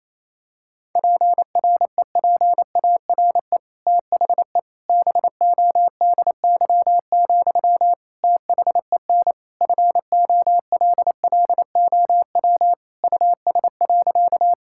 Morse code, heard as a keyed tone, about 28 wpm, 700 Hz. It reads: PREPARE THE BODY, THEN FOLLOW US.
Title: Sentences from Top 1000 Words - Review Part 6 - 28wpm